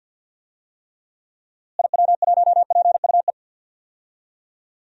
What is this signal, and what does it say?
Morse code, heard as a keyed tone, about 52 wpm, 700 Hz. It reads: IW1PUE
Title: DXCC-100 Call Signs - 6 Characters - Repeated 3x - 50wpm